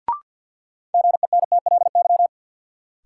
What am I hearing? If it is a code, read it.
Morse code, heard as a keyed tone, about 50 wpm, 700 Hz. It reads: GENTLY
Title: Sentences from Top 1500 Words - New Word Drill Alt - 50wpm